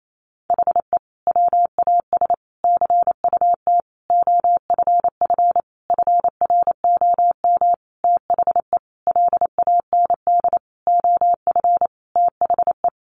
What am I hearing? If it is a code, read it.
Morse code, heard as a keyed tone, about 28 wpm, 700 Hz. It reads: HE WAS CUT OFF FROM THE LAND OF THE
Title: Sentences from Top 1100 Words 28wpm